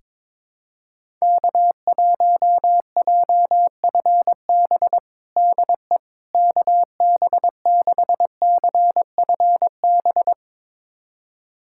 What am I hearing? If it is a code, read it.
Morse code, heard as a keyed tone, about 22 wpm, 700 Hz. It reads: K1JFB DE KB6CFB
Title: USA Callsign DE Callsign 22wpm